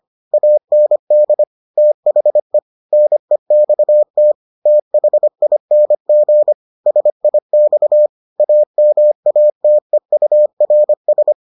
Morse code, heard as a keyed tone, about 25 wpm, 600 Hz. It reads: AND THE NEXT THING SIX AMATEURS